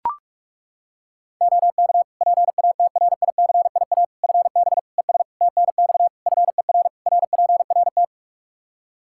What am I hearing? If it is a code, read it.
Morse code, heard as a keyed tone, about 41 wpm, 700 Hz. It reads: OK PATRICIA FB ES TNX FER RPRT
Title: Practice QSO - Exchange 2 - Info - 45wpm